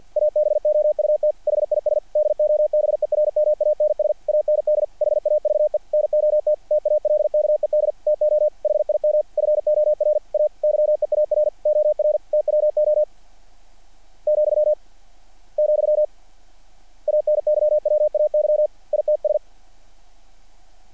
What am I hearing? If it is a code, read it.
Morse code, heard as a keyed tone, about 50 wpm, 600 Hz. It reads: ABOUT HIS DOBERMANS AND HAVE NOT TALKED TO HIM FOR A YEAR OR TWO , , ANYWAY ITS
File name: archived / CRSnet